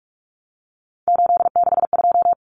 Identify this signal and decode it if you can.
Morse code, heard as a keyed tone, about 44 wpm, 700 Hz. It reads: 862
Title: Common Cut Numbers - 3-Digit 45wpm